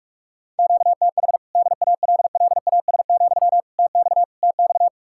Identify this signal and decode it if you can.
Morse code, heard as a keyed tone, about 44 wpm, 700 Hz. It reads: QTH DALLAS, TX TX